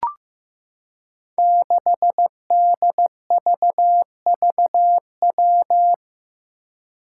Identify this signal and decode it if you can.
Morse code, heard as a keyed tone, about 15 wpm, 700 Hz. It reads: 6DVVW
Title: Five Challenging Letter-Number Combinations 15wpm